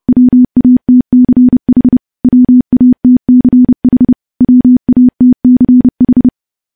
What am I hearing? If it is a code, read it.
Morse code, heard as a keyed tone, about 30 wpm, 250 Hz. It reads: WATCH WATCH WATCH
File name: morse / morse-training-30WPM